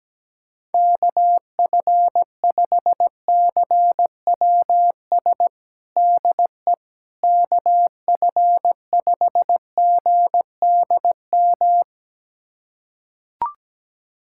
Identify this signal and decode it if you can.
Morse code, heard as a keyed tone, about 17 wpm, 700 Hz. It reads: KF5CWS DE KF5GDM
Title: USA Callsign DE Callsign - 6 Characters - 17wpm